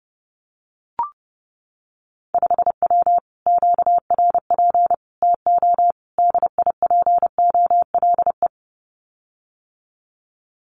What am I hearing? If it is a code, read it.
Morse code, heard as a keyed tone, about 30 wpm, 700 Hz. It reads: 5W QRP TO DIPOLE
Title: Common QSO Phrases Encore 30wpm